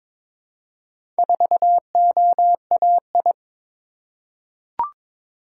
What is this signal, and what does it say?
Morse code, heard as a keyed tone, about 22 wpm, 700 Hz. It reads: 4OAI
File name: practice / 22wpm / Four Letter-Number Combinations - 22wpm